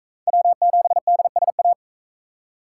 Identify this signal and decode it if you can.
Morse code, heard as a keyed tone, about 41 wpm, 700 Hz. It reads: W7DSA